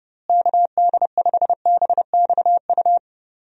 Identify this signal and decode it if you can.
Morse code, heard as a keyed tone, about 30 wpm, 700 Hz. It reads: KD5BXU
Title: USA Call Signs - 6 Characters - Repeated 3x - 30wpm